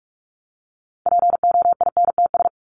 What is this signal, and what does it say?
Morse code, heard as a keyed tone, about 44 wpm, 700 Hz. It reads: POINTS
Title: ES:S Suffix - Top 100 Words with Base Words 45wpm